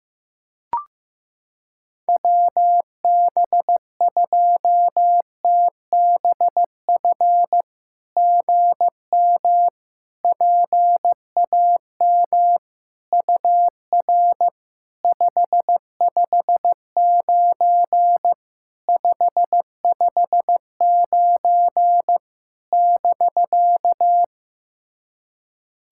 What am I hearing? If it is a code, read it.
Morse code, heard as a keyed tone, about 15 wpm, 700 Hz. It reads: WB2TBF GM PAM UR 559 559 <BK>
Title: SOTA QSO Practice 15wpm